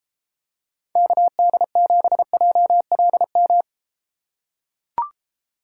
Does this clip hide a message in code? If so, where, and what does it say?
Yes: Morse code from 1.0 to 3.6 s, heard as a keyed tone, about 31 wpm, 700 Hz. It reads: KD7JLM